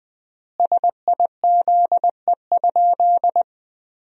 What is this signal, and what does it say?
Morse code, heard as a keyed tone, about 20 wpm, 700 Hz. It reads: SIZE?
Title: 4x Word-Spacing - Sentences from Top 500 Words - 20wpm